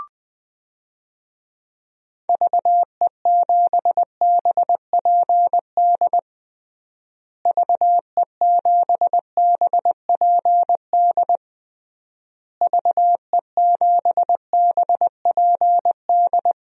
Morse code, heard as a keyed tone, about 20 wpm, 700 Hz. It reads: VE7BPD VE7BPD VE7BPD
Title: DXCC-100 Call Signs - 6 Characters - Repeated 3x - 20wpm